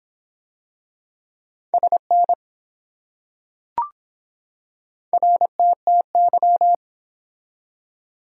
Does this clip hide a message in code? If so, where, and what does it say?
Yes: Morse code from 1.7 to 6.7 s, heard as a keyed tone, about 23 wpm, 700 Hz. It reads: SN RTTY